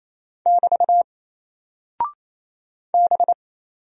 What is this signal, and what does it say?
Morse code, heard as a keyed tone, about 28 wpm, 700 Hz. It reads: <BT> B
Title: Single Character (Letter:Number:Punctuation:Prosign) Alt 28wpm